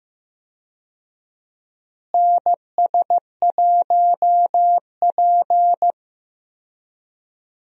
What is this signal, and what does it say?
Morse code, heard as a keyed tone, about 15 wpm, 700 Hz. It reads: NS1P
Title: USA Call Signs - 4 Characters - Repeated 3x - 15wpm